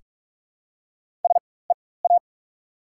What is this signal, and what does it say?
Morse code, heard as a keyed tone, about 44 wpm, spaced as for 21 wpm, 700 Hz. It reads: SEA
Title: 4x Character Spacing - Top 300 Words - 45wpm